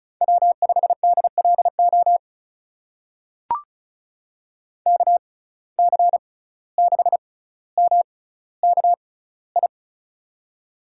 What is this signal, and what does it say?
Morse code, heard as a keyed tone, about 36 wpm, spaced as for 17 wpm, 700 Hz. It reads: W5DLO KC6MKI
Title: USA Call Signs - 6x Character Spacing - 35wpm